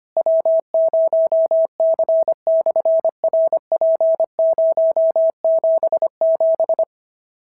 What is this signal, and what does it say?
Morse code, heard as a keyed tone, about 25 wpm, 650 Hz. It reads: W0C/RP077
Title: SOTA QSO Practice 25wpm